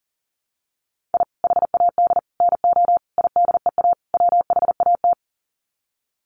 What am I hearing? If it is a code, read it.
Morse code, heard as a keyed tone, about 40 wpm, 700 Hz. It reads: I HAD NO IDEA WHAT
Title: Sets of 5 Words Encore 40wpm